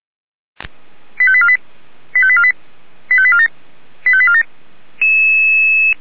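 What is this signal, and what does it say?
Five-tone selective call (ZVEI), heard as quick runs of five tones, tones of about 70 ms. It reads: 85848 85848 85847 85847
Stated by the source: ZVEI five-tone selective call received via SDR, Feuerwehr Orlamünde.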